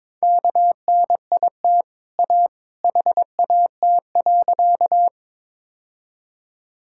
Morse code, heard as a keyed tone, about 22 wpm, 700 Hz. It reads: KNIT A HAT.
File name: practice / 22wpm / Easy Sentences with Hints - 22wpm